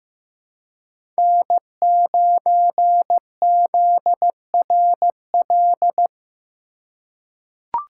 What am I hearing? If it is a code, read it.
Morse code, heard as a keyed tone, about 15 wpm, 700 Hz. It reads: N9ZRL
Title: USA Call Signs - 3x Character Spacing - 15wpm